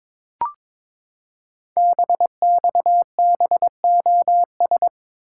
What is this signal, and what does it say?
Morse code, heard as a keyed tone, about 22 wpm, 700 Hz. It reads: BXBOS
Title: Code Groups - 5 Letters - 22wpm